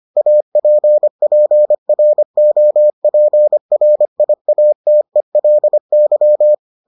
Morse code, heard as a keyed tone, about 25 wpm, 600 Hz. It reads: APPROPRIATELY